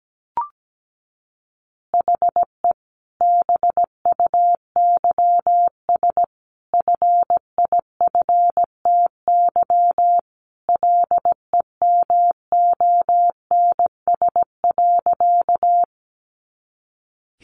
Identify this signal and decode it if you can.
Morse code, heard as a keyed tone, about 17 wpm, 700 Hz. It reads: HE BUYS FIFTY LEMONS.
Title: Additional Easy Phrases 17wpm